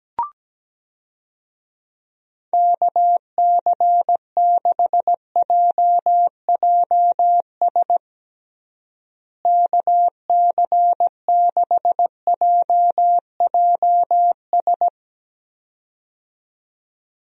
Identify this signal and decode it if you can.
Morse code, heard as a keyed tone, about 17 wpm, 700 Hz. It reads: KC6JJS KC6JJS
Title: USA Call Signs - 6 Characters - Repeated 2x - 17wpm